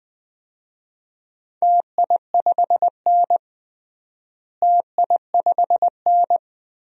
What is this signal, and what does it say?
Morse code, heard as a keyed tone, about 20 wpm, 700 Hz. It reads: TI5N TI5N
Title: DXCC-100 Call Signs - 4 Characters - Repeated 2x - 20wpm